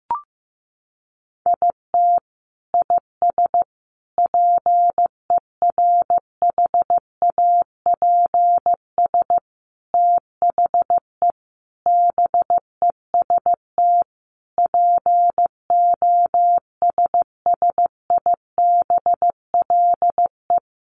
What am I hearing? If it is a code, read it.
Morse code, heard as a keyed tone, about 15 wpm, 700 Hz. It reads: IT IS PERHAPS THE BEST POSSIBLE
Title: Sentences from Top 1300 Words 15wpm